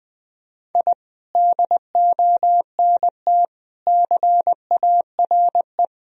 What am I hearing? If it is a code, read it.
Morse code, heard as a keyed tone, about 21 wpm, 700 Hz. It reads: I DONT CARE
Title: Sentences from Top 1000 Words - Review Part 3 - 20wpm